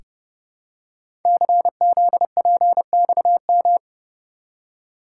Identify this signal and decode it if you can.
Morse code, heard as a keyed tone, about 29 wpm, 700 Hz. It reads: CZPXM
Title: Code Groups - 5 Letters - 30wpm